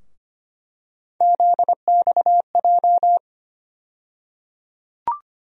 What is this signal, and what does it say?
Morse code, heard as a keyed tone, about 25 wpm, 700 Hz. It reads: ZXJ
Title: USA Call Sign Postfixes - 3 Letters 25wpm